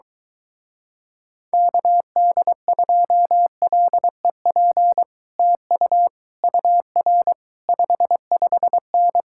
Morse code, heard as a keyed tone, about 22 wpm, 700 Hz. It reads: KD2LEP TU UR 55N